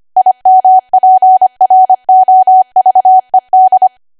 Morse code, heard as a keyed tone, about 25 wpm, 750 Hz. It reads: IMPROVED